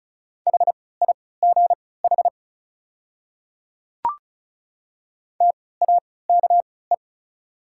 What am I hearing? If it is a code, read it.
Morse code, heard as a keyed tone, about 36 wpm, spaced as for 20 wpm, 690 Hz. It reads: HIGH TAKE
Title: 3x Character Spacing - Top 200 Words - 35wpm